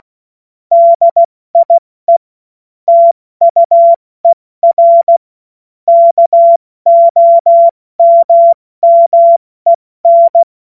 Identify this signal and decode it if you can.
Morse code, heard as a keyed tone, about 16 wpm, spaced as for 14 wpm, 680 Hz. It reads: DIE TUER KOMMEN
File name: Morsekurs / Witze / Witz 10. Mai 2022 70 F80